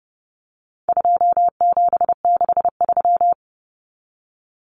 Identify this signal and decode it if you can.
Morse code, heard as a keyed tone, about 30 wpm, 700 Hz. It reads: 2763
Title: Common Cut Numbers - 4-Digit 30wpm